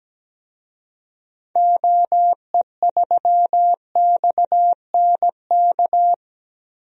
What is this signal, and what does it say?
Morse code, heard as a keyed tone, about 17 wpm, 700 Hz. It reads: OE3XNK
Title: DXCC-100 Call Signs - 6 Characters - Repeated 3x - 17wpm